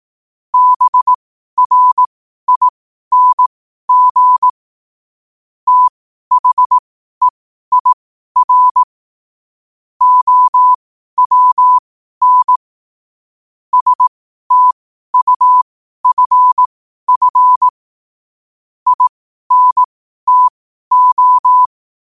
Morse code, heard as a keyed tone, about 18 wpm, spaced as for 12 wpm, 1000 Hz. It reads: BRING THEIR OWN STUFF INTO
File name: qotdfiles / 12wpm